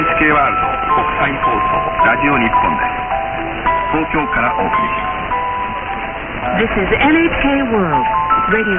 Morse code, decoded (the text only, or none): none